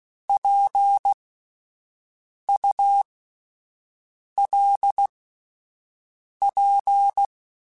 Morse code, heard as a keyed tone, about 16 wpm, spaced as for 5 wpm, 800 Hz. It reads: PULP